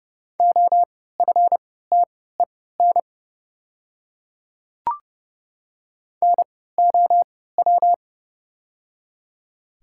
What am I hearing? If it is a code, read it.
Morse code, heard as a keyed tone, about 32 wpm, spaced as for 17 wpm, 700 Hz. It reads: OFTEN NOW
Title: 3x Character Spacing - Top 300 Words - 30wpm